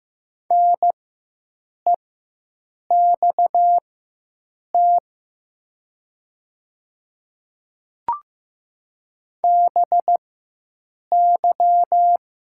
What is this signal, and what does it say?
Morse code, heard as a keyed tone, about 15 wpm, spaced as for 7 wpm, 700 Hz. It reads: NEXT BY
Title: 4x Character Spacing - Top 300 Words - 15wpm